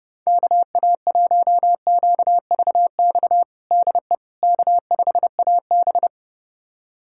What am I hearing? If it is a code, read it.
Morse code, heard as a keyed tone, about 32 wpm, 700 Hz. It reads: KA1QVX DE K5AB
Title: USA Callsign DE Callsign 30wpm